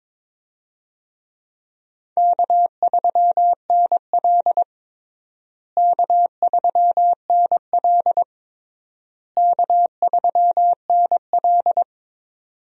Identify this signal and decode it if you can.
Morse code, heard as a keyed tone, about 21 wpm, 700 Hz. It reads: K3NL K3NL K3NL
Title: USA Call Signs - 4 Characters - Repeated 3x - 22wpm